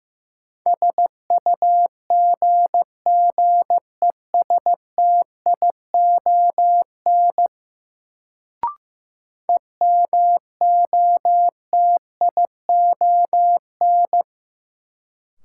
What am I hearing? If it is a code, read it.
Morse code, heard as a keyed tone, about 15 wpm, 700 Hz. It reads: SUGGESTION EMOTION
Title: Sentences from Top 1600 Words - New Word Drill - 15wpm